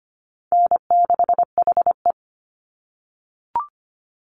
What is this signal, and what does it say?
Morse code, heard as a keyed tone, about 25 wpm, 700 Hz. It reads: N6HE
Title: Call Signs - 25wpm